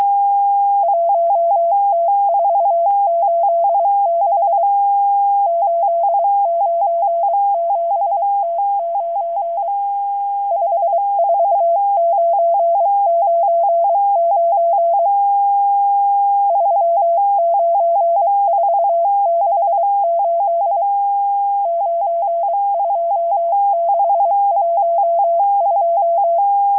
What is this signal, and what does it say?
Morse code, heard as a keyed tone, about 23 wpm, 680 Hz. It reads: 1T486 897T9 54999 39468 92612
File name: HM02- -0437utc-7351khz